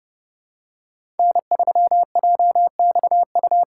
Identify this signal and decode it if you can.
Morse code, heard as a keyed tone, about 30 wpm, 700 Hz. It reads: N3JXU
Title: USA Call Signs - 5 Characters - 6x Character Spacing - 30wpm